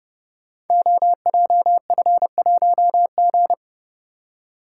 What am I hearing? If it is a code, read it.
Morse code, heard as a keyed tone, about 30 wpm, 700 Hz. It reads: OJF1G
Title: Code Groups - 5 Letter:Numbers - 30wpm